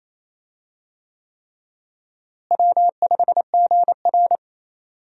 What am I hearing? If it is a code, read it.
Morse code, heard as a keyed tone, about 28 wpm, 700 Hz. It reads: W5GR
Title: USA Call Signs - 4 Characters - Repeated 3x - 28wpm